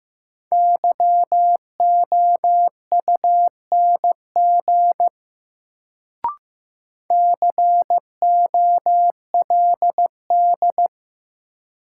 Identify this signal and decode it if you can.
Morse code, heard as a keyed tone, about 15 wpm, 700 Hz. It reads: YOUNG COLD
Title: Sentences from Top 400 Words - New Word Drill - 15wpm